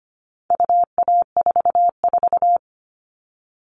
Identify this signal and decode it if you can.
Morse code, heard as a keyed tone, about 25 wpm, 700 Hz. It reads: UA44
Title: All Cut Numbers - 4-Digit 25wpm